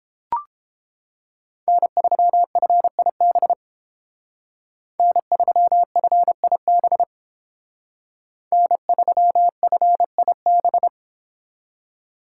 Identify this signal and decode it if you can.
Morse code, heard as a keyed tone, about 31 wpm, 700 Hz. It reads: N3FIB N3FIB N3FIB